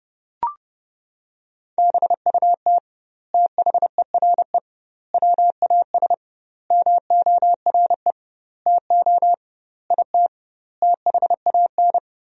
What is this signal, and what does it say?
Morse code, heard as a keyed tone, about 30 wpm, 700 Hz. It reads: BUT THERE WAS MORE TO IT THAN
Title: 2x Word-Spacing - Sentences from Top 100 Words - 30wpm